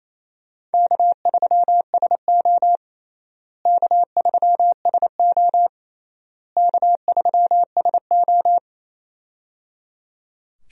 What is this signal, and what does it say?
Morse code, heard as a keyed tone, about 28 wpm, 700 Hz. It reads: K3SO K3SO K3SO
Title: USA Call Signs - 4 Characters - Repeated 3x - 28wpm